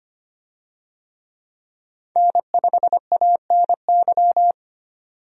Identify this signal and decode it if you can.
Morse code, heard as a keyed tone, about 25 wpm, 700 Hz. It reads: N5ANY
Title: USA Call Signs - 5 Characters - 25wpm